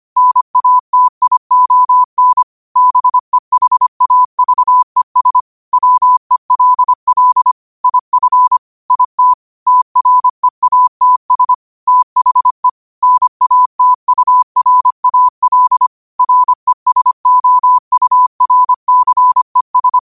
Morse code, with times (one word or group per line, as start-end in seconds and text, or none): 0.2-2.4 NATION
2.8-5.4 BEHAVES
5.7-7.5 WELL
7.8-8.6 IF
8.9-9.3 IT
9.7-11.5 TREATS
11.9-12.7 THE
13.0-15.9 NATURAL
16.2-20.0 RESOURCES